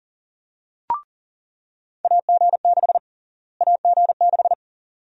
Morse code, heard as a keyed tone, about 40 wpm, 700 Hz. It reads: AG6 AG6
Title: USA Call Sign Prefixes - Repeated 2x - 40wpm